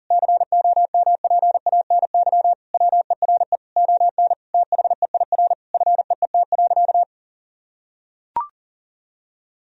Morse code, heard as a keyed tone, about 40 wpm, 700 Hz. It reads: COMPANY WERE ON THEIR FEET.
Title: Sentences from Top 1000 Words - Review Part 3 - 40wpm